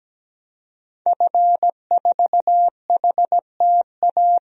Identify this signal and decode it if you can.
Morse code, heard as a keyed tone, about 17 wpm, 700 Hz. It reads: F4HTA
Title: DXCC-100 Call Signs - 5 Characters - 17wpm